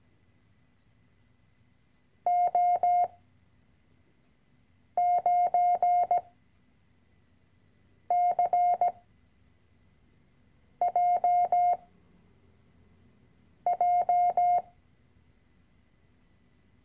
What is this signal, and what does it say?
Morse code, heard as a keyed tone, about 17 wpm, 710 Hz. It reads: O 9 C J J
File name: CW-Online-Kurs / Anfänger / Lektion 010